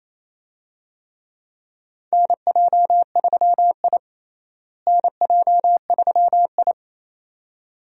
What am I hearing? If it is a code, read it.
Morse code, heard as a keyed tone, about 28 wpm, 700 Hz. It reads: NJ3I NJ3I
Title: USA Call Signs - 4 Characters - Repeated 2x - 28wpm